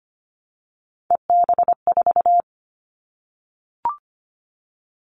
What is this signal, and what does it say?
Morse code, heard as a keyed tone, about 25 wpm, 700 Hz. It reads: EB4
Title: Callsign Prefixes for Top 25 DXCC - 25wpm